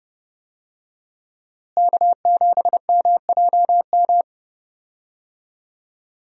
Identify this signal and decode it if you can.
Morse code, heard as a keyed tone, about 30 wpm, 700 Hz. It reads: K7MJM
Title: USA Call Signs - 5 Characters - Repeated 3x - 30wpm